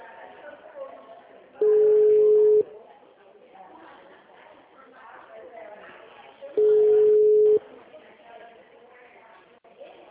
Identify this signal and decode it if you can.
Ringback tone (Europe, 425 Hz), with speech.